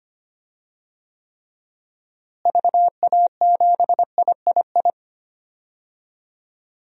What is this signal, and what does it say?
Morse code, heard as a keyed tone, about 25 wpm, 700 Hz. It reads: VA7III
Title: DXCC-100 Call Signs - 6 Characters - Repeated 3x - 25wpm